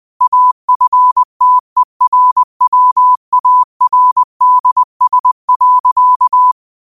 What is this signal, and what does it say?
Morse code, heard as a keyed tone, about 20 wpm, 1000 Hz. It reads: AFTERWARDS.
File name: qotdfiles / 20wpm